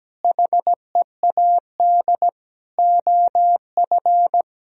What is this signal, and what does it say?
Morse code, heard as a keyed tone, about 17 wpm, 700 Hz. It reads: HEAD OF
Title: Sentences from Top 1000 Words - Review Part 2 - 17wpm